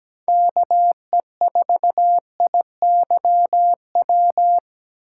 Morse code, heard as a keyed tone, about 17 wpm, 700 Hz. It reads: KE4IYW
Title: USA Call Signs - 6 Characters - 3x Character Spacing - 17wpm